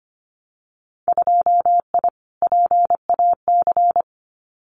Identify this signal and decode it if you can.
Morse code, heard as a keyed tone, about 25 wpm, 700 Hz. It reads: 2I PAC
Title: Winter Field Day Exchanges Encore 25wpm